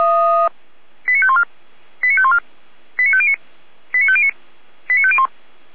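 Five-tone selective call (ZVEI), heard as quick runs of five tones, tones of about 70 ms. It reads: 89514 89514 89509 89509 89501
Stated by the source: ZVEI five-tone selective call received on an SDR, fire brigade alerting.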